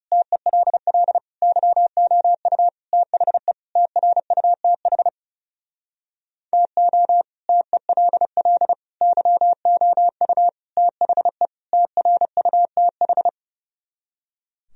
Morse code, heard as a keyed tone, about 40 wpm, 700 Hz. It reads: TELL YOU THE TRUTH TO TELL YOU THE TRUTH